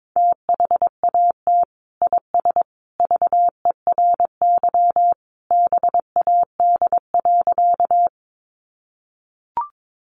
Morse code, heard as a keyed tone, about 22 wpm, 700 Hz. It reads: THAT IS VERY BAD.
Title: Sentences from Top 700 Words 22wpm